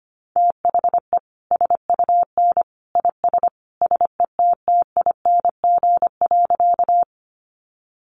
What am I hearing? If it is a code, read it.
Morse code, heard as a keyed tone, about 25 wpm, 700 Hz. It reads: THE SUN IS SETTING.
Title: Sentences from Top 1300 Words 25wpm